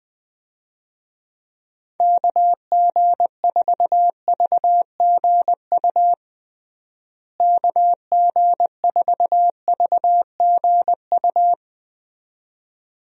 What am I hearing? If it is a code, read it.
Morse code, heard as a keyed tone, about 20 wpm, 700 Hz. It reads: KG4VGU KG4VGU
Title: USA Call Signs - 6 Characters - Repeated 2x - 20wpm